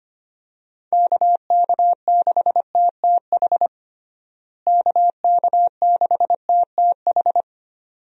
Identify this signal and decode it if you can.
Morse code, heard as a keyed tone, about 25 wpm, 700 Hz. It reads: KK6TTH KK6TTH
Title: USA Call Signs - 6 Characters - Repeated 2x - 25wpm